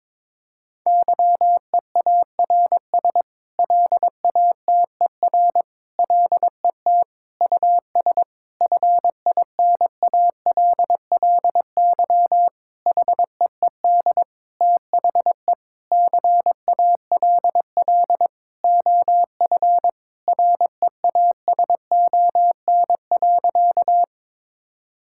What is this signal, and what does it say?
Morse code, heard as a keyed tone, about 22 wpm, 700 Hz. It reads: YEARS LATER LET US FINALLY HEED THE CALL OF REASON.
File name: practice / 22wpm / Sentences with 8-Word Phrases - 22wpm